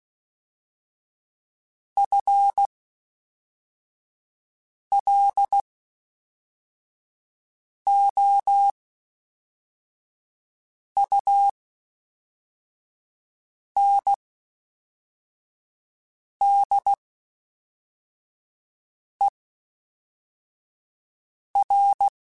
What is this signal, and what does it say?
Morse code, heard as a keyed tone, about 16 wpm, spaced as for 4 wpm, 800 Hz. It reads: FLOUNDER
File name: Spaced / FL Runs sp mp3s / FL Runs sp @5